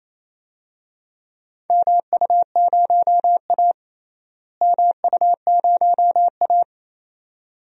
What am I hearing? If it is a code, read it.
Morse code, heard as a keyed tone, about 28 wpm, 700 Hz. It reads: MU0A MU0A
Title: DXCC-100 Call Signs - 4 Characters - Repeated 2x - 28wpm